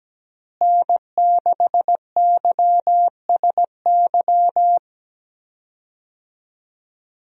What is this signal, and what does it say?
Morse code, heard as a keyed tone, about 17 wpm, 700 Hz. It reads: N6YSY
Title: USA Call Signs - 5 Characters - Repeated 2x - 17wpm